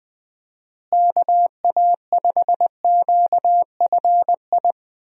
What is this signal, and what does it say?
Morse code, heard as a keyed tone, about 20 wpm, 700 Hz. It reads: KA5QFI
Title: USA Call Signs - 6 Characters - Repeated 3x - 20wpm